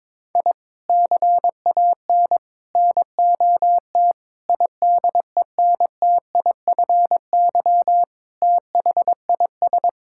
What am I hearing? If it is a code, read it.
Morse code, heard as a keyed tone, about 21 wpm, 700 Hz. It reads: I CAN NOT IDENTIFY THIS
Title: Sentences from Top 1000 Words - Review Part 6 - 22wpm